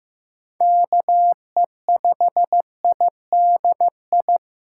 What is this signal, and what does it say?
Morse code, heard as a keyed tone, about 16 wpm, 700 Hz. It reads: KE5IDI
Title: USA Call Signs - 6 Characters - 4x Character Spacing - 15wpm